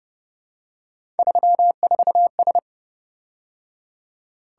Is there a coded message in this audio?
Yes: Morse code from 1.2 to 2.6 s, heard as a keyed tone, about 30 wpm, 700 Hz. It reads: 34S